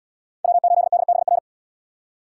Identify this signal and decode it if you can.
Morse code, heard as a keyed tone, about 68 wpm, 700 Hz. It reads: W8DBR